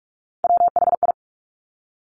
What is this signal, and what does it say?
Morse code, heard as a keyed tone, about 46 wpm, 700 Hz. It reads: WHI
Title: Top Three Letter Combinations in Words 45wpm